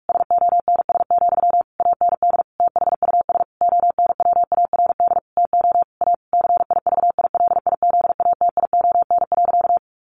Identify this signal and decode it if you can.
Morse code, heard as a keyed tone, about 44 wpm, 700 Hz. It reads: SONS, AND THUS ONWARD TO A CIVILIZATION.
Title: A Sound Of Thunder 45wpm